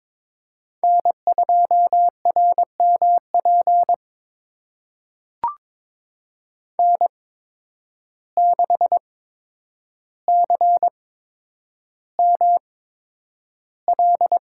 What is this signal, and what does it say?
Morse code, heard as a keyed tone, about 22 wpm, spaced as for 9 wpm, 700 Hz. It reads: N2RMP N6CML
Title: USA Call Signs - 8x Character Spacing - 22wpm